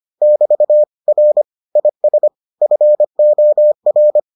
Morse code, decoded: <BT> R IS FOR

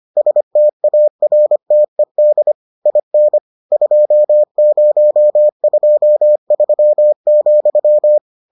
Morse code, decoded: STARTED IN 2023,

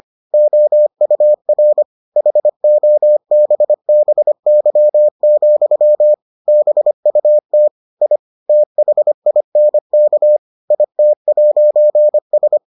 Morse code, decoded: OUR HOBBY, BUT I THINK IT'S

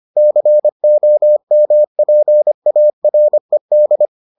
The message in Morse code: COMPARED

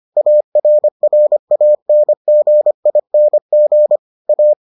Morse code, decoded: ARRANGING A

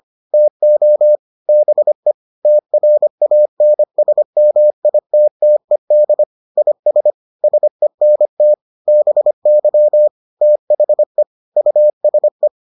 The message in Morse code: TO BE TRANSMITTED IS SENT BY THE USE

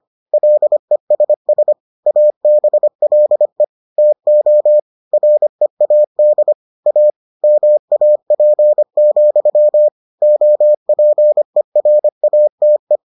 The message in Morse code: LESS ABLE TO READ A MAP, OPERATE

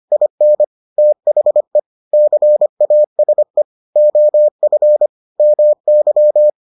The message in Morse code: IN THE CASE OF MY

